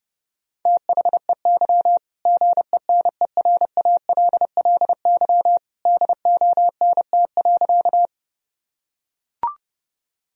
Morse code, heard as a keyed tone, about 31 wpm, 700 Hz. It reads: THEY GENERALLY DONT.
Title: Sentences from Top 1000 Words - Review Part 4 - 30wpm